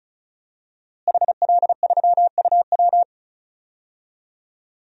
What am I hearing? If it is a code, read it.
Morse code, heard as a keyed tone, about 35 wpm, 700 Hz. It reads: HL3UW
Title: DXCC-100 Call Signs - 5 Characters - Repeated 2x - 35wpm